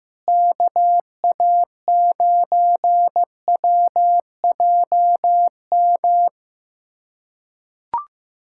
Morse code, heard as a keyed tone, about 16 wpm, 700 Hz. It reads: KA9WJM
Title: USA Callsign DE Callsign 15wpm